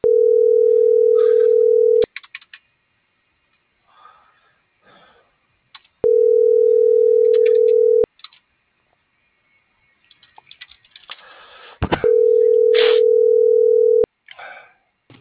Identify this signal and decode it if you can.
Ringback tone (North America), 440 + 480 Hz.